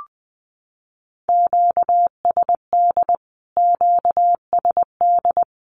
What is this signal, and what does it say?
Morse code, heard as a keyed tone, about 20 wpm, 700 Hz. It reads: QSD QSD
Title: Q-Codes Repeated 3x 20wpm